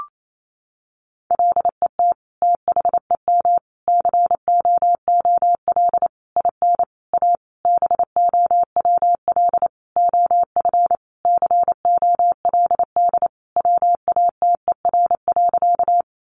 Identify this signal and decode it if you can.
Morse code, heard as a keyed tone, about 28 wpm, 700 Hz. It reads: LET THEM COOL IN A BOWL OF COLD WATER.
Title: Sentences from Top 1200 Words 28wpm